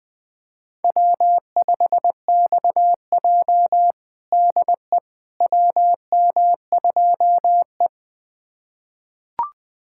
Morse code, heard as a keyed tone, about 20 wpm, 700 Hz. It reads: W5XJ DE WM2E
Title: USA Callsign DE Callsign - 4 Characters - 20wpm